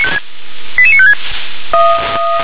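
Five-tone selective call (ZVEI), heard as quick runs of five tones, tones of about 70 ms. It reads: 88956
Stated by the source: ZVEI five-tone selective call received on SDR.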